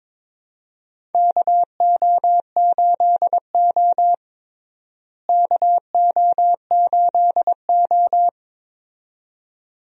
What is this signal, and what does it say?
Morse code, heard as a keyed tone, about 22 wpm, 700 Hz. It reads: KO8O KO8O
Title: USA Call Signs - 4 Characters - Repeated 2x - 22wpm